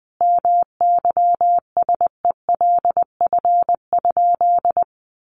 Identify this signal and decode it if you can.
Morse code, heard as a keyed tone, about 20 wpm, 700 Hz. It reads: MYSELF?
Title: Sentences from Top 1300 Words 20wpm